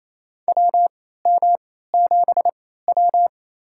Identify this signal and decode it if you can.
Morse code, heard as a keyed tone, about 28 wpm, spaced as for 16 wpm, 700 Hz. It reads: WM7W
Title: USA Call Signs - 3x Character Spacing - 28wpm